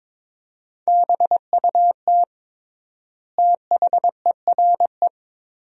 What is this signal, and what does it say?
Morse code, heard as a keyed tone, about 22 wpm, 700 Hz. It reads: BUT THERE
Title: 3x Word-Spacing - Sentences from Top 100 Words - 22wpm